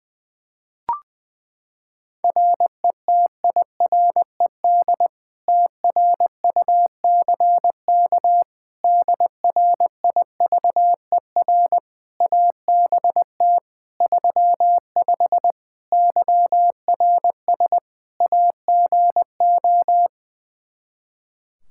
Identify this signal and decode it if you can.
Morse code, heard as a keyed tone, about 20 wpm, 700 Hz. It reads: RETIRED TRUCK DRIVER ABT 35 YRS AGO
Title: Practice QSO - Exchange 3 - More Info - 20wpm